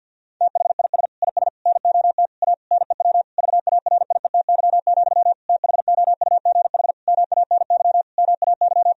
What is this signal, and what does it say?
Morse code, heard as a keyed tone, about 50 wpm, 700 Hz. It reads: THIS IS NOT A NEW VARIETY, THOUGH MANY MAY
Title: Sentences from Top 1000 Words - Review Part 4 - 50wpm